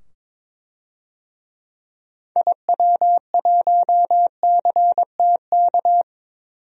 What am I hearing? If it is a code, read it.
Morse code, heard as a keyed tone, about 22 wpm, 700 Hz. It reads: IW1CTK
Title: DXCC-100 Call Signs - 6 Characters - 22wpm